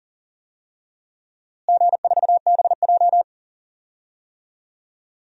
Morse code, heard as a keyed tone, about 40 wpm, 700 Hz. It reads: G4BJ
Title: DXCC-100 Call Signs - 4 Characters - Repeated 2x - 40wpm